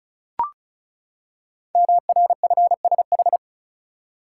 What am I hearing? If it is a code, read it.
Morse code, heard as a keyed tone, about 36 wpm, 700 Hz. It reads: MRFSH
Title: Code Groups - 5 Letters - 35wpm